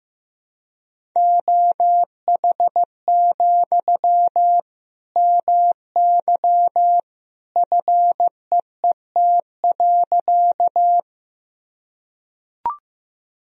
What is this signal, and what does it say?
Morse code, heard as a keyed tone, about 15 wpm, 700 Hz. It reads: OH, MY FEET.
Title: Sentences from Top 1000 Words - Review Part 1 - 15wpm